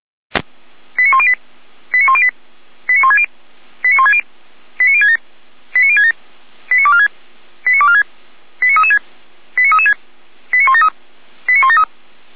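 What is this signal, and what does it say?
Five-tone selective call (ZVEI), heard as quick runs of five tones, tones of about 70 ms. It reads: 89108 89108 89160 89160 89067 89067 89246 89246 89336 89336 89173 89173
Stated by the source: ZVEI five-tone selective call received on SDR, from audio.f85360.feuerwehr-orlamuende.org (Feuerwehr Orlamünde fire service).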